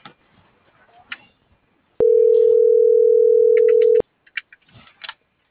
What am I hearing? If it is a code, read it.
Ringback tone (North America), 440 + 480 Hz.